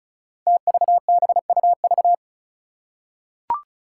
Morse code, heard as a keyed tone, about 34 wpm, 700 Hz. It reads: TVBUV
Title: Five Challenging Letter-Number Combinations 35wpm